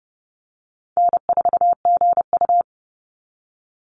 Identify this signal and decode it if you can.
Morse code, heard as a keyed tone, about 30 wpm, 700 Hz. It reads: N4GU
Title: Easy Call Signs - 30wpm